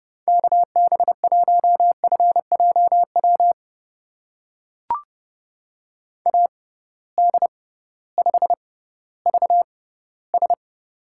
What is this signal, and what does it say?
Morse code, heard as a keyed tone, about 30 wpm, spaced as for 16 wpm, 700 Hz. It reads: KB1FJW AD5VS
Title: USA Call Signs - 6x Character Spacing - 30wpm